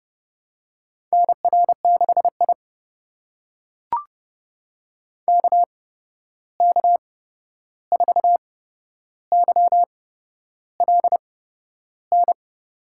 Morse code, heard as a keyed tone, about 32 wpm, spaced as for 11 wpm, 700 Hz. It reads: NR6I KK4YLN
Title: USA Call Signs - 8x Character Spacing - 30wpm